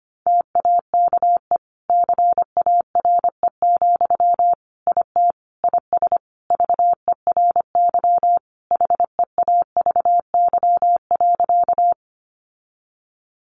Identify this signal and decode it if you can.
Morse code, heard as a keyed tone, about 25 wpm, 700 Hz. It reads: TAKE CARE, IT IS VERY HEAVY.
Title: Sentences from Top 1100 Words 25wpm